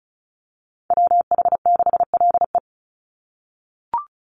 Morse code, heard as a keyed tone, about 34 wpm, 700 Hz. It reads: WH6LE
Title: NA SOTA Activator Callsigns 35wpm